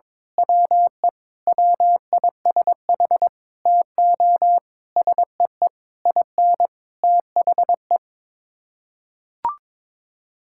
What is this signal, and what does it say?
Morse code, heard as a keyed tone, about 22 wpm, 700 Hz. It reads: WE WISH TO SEE IN THE
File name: Sets of 9 Words - 22wpm